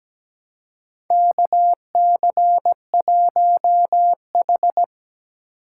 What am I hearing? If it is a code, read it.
Morse code, heard as a keyed tone, about 17 wpm, 700 Hz. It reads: KC1H
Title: USA Call Signs - 4 Characters - 3x Character Spacing - 17wpm